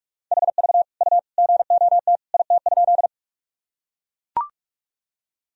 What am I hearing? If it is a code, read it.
Morse code, heard as a keyed tone, about 43 wpm, 700 Hz. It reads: HV U GOT IT?
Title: Sentences from Top 300 Words and Abbreviations 45wpm